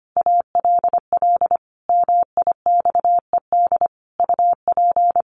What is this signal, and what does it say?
Morse code, heard as a keyed tone, about 25 wpm, 700 Hz. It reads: ALL MIXED UP
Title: Sentences from Top 1500 Words 25wpm